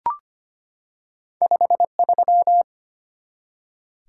Morse code, heard as a keyed tone, about 25 wpm, 700 Hz. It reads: 53
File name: Two Letter-Number Combinations - 25wpm